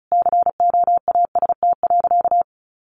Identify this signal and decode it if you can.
Morse code, heard as a keyed tone, about 34 wpm, 700 Hz. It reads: COAST.
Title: Sentences from Top 1300 Words 35wpm